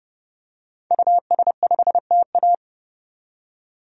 Morse code, heard as a keyed tone, about 30 wpm, 700 Hz. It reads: US5TA
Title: DXCC-100 Call Signs - 5 Characters - 30wpm